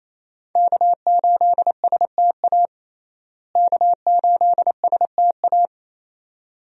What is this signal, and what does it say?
Morse code, heard as a keyed tone, about 27 wpm, 700 Hz. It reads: K8STA K8STA
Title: USA Call Signs - 5 Characters - Repeated 2x - 28wpm